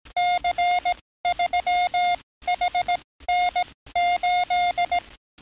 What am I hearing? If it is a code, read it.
Morse code, heard as a keyed tone, about 18 wpm, spaced as for 14 wpm, 710 Hz. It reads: C3HN8
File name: training / NAVPERS 11049to11060 / NAVPERS 11057 RA RB